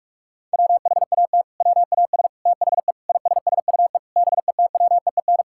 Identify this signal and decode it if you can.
Morse code, heard as a keyed tone, about 46 wpm, 700 Hz. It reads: WHAT WAS THE ISSUE BETWEEN
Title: Sentences from Top 1000 Words - Review Part 6 - 45wpm